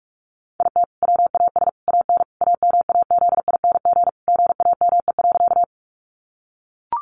Morse code, heard as a keyed tone, about 44 wpm, 700 Hz. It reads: IT WAS AN AMAZING GAME.
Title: Sentences from Top 1500 Words 45wpm